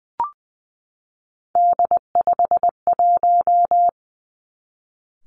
Morse code, heard as a keyed tone, about 20 wpm, 700 Hz. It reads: D51
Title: Three Challenging Letter-Number Combinations 20wpm